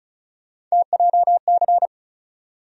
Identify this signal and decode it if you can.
Morse code, heard as a keyed tone, about 36 wpm, 700 Hz. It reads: TJC
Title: USA Call Sign Postfixes - 3 Letters 35wpm